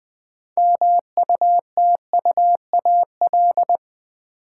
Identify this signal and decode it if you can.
Morse code, heard as a keyed tone, about 20 wpm, 700 Hz. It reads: MUTUAL
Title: 6 Letter Words - First 3 Letters Given - 20wpm